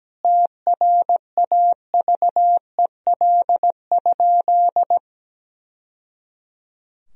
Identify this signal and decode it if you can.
Morse code, heard as a keyed tone, about 17 wpm, 700 Hz. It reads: TRAVEL?
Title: Sentences from Top 1000 Words - Review Part 2 - 17wpm